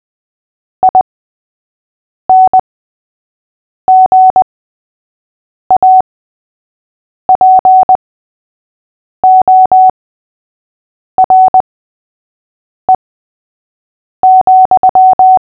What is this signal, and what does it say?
Morse code, heard as a keyed tone, about 20 wpm, spaced as for 6 wpm, 750 Hz. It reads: INGAPORE,
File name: cw / fists / Short-Story-3-6wpm